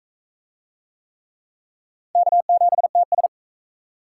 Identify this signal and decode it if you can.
Morse code, heard as a keyed tone, about 42 wpm, 700 Hz. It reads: K7TS